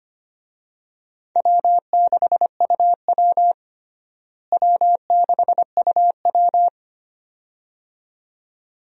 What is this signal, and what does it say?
Morse code, heard as a keyed tone, about 25 wpm, 700 Hz. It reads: W6UW W6UW
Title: USA Call Signs - 4 Characters - Repeated 2x - 25wpm